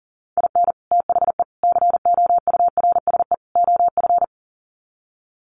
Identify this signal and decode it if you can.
Morse code, heard as a keyed tone, about 40 wpm, 700 Hz. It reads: IN THE COURSE OF
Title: Sets of 4 Words 40wpm